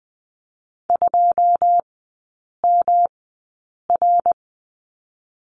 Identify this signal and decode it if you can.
Morse code, heard as a keyed tone, about 20 wpm, spaced as for 8 wpm, 700 Hz. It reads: 2MR
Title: 138 Lesson 23e - Letter f - Three Characters 20wpm